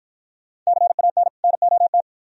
Morse code, heard as a keyed tone, about 53 wpm, 690 Hz. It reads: CAN NOT